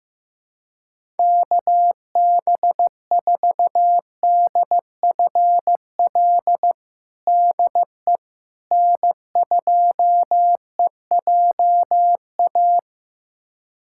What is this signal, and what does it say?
Morse code, heard as a keyed tone, about 16 wpm, 700 Hz. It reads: KB4DFL DE N2EJA
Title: USA Callsign DE Callsign 15wpm